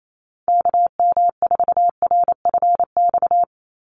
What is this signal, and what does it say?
Morse code, heard as a keyed tone, about 28 wpm, 700 Hz. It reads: KM4RFX
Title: USA Call Signs 28wpm